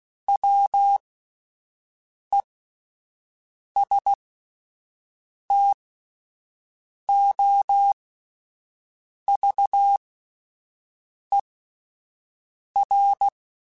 Morse code, heard as a keyed tone, about 16 wpm, spaced as for 5 wpm, 800 Hz. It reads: WESTOVER